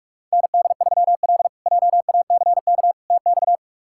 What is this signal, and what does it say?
Morse code, heard as a keyed tone, about 41 wpm, 700 Hz. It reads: ND3L JACK TX
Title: K1USN SST QSO Practice - 45wpm